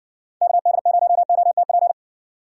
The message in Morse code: KD0OTP

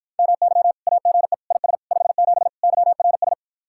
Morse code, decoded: MY AGE IS 57 YRS

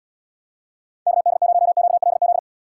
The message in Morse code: KD0QLZ

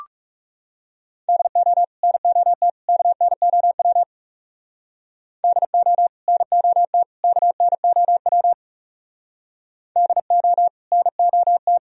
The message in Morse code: DO NOT KNOW DO NOT KNOW DO NOT

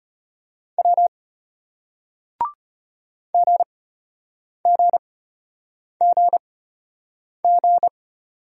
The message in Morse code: W GGGG

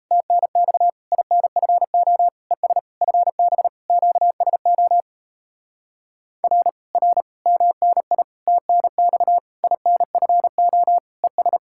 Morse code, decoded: TNX INFO ES FB QSO R R MNI TNX INFO ES

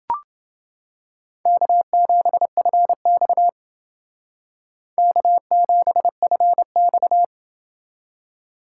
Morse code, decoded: K7FX K7FX

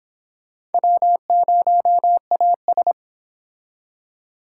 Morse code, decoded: W0AS

